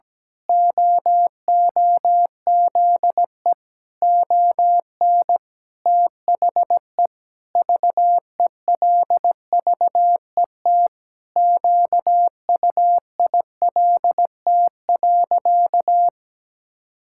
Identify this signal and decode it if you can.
Morse code, heard as a keyed tone, about 17 wpm, 700 Hz. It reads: OOZE ON THE VELVET QUILT.
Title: Pangram Challenge v2 - 17wpm